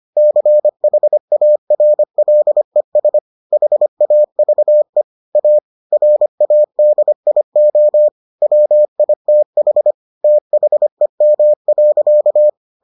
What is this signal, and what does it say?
Morse code, heard as a keyed tone, about 25 wpm, 600 Hz. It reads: CHARLES HAVE A RADIO WITH THEM.